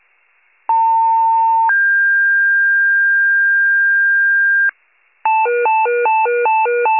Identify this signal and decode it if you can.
Two-tone page, heard as a one-second tone, then a three-second tone, about 900 then 1600 Hz, with noise. It is heard from 0.7 to 4.7 s.